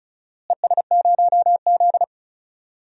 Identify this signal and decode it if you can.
Morse code, heard as a keyed tone, about 36 wpm, 700 Hz. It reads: ES0Z
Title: DXCC-100 Call Signs - 4 Characters - Repeated 2x - 35wpm